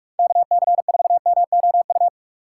Morse code, heard as a keyed tone, about 46 wpm, 700 Hz. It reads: KC4MOU
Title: USA Call Signs - 2x Character Spacing - 45wpm